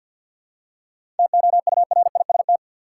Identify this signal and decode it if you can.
Morse code, heard as a keyed tone, about 46 wpm, 700 Hz. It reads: TOURIST
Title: Seven Letter Words - 50wpm